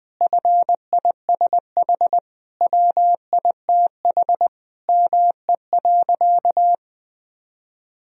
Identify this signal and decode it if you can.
Morse code, heard as a keyed tone, about 20 wpm, 700 Hz. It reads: FISH WITH ME.
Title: Sentences from Top 1000 Words - Review Part 1 - 20wpm